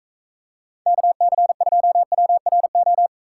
Morse code, heard as a keyed tone, about 45 wpm, 700 Hz. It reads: KC2WRO